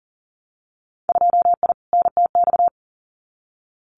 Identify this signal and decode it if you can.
Morse code, heard as a keyed tone, about 40 wpm, 700 Hz. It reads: 2I NTX
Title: Winter Field Day Exchanges Encore 40wpm